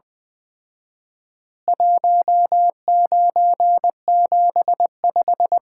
Morse code, heard as a keyed tone, about 20 wpm, 700 Hz. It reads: 1975
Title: Common QSO Phrases - 4x Word-Spacing - 20wpm